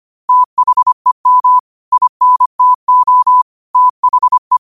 Morse code, heard as a keyed tone, about 25 wpm, 1000 Hz. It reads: THEM INTO THE
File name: 25wpm